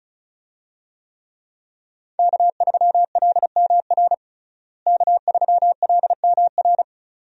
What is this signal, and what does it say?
Morse code, heard as a keyed tone, about 36 wpm, 700 Hz. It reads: K3LMR K3LMR
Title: USA Call Signs - 5 Characters - Repeated 2x - 35wpm